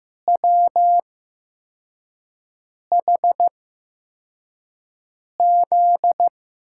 Morse code, heard as a keyed tone, about 15 wpm, spaced as for 4 wpm, 700 Hz. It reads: WHZ
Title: USA Call Signs - 5 Characters - 8x Character Spacing - 15wpm